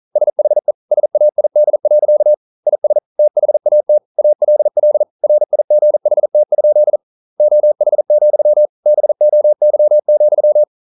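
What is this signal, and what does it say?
Morse code, heard as a keyed tone, about 42 wpm, 600 Hz. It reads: SHE SAID. IS THAT ALL RIGHT? OH, BOY,
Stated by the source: Morse code practice at 41 wpm, from CW med Gnister.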